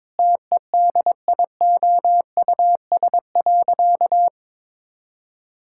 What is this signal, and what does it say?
Morse code, heard as a keyed tone, about 22 wpm, 700 Hz. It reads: TEDIOUS.
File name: Sentences with 7-Word Phrases - 22wpm